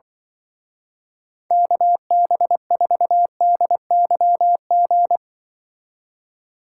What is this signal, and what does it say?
Morse code, heard as a keyed tone, about 24 wpm, 700 Hz. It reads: KB4DYG
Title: Speed Racing - USA 2x3 Call Signs 20wpm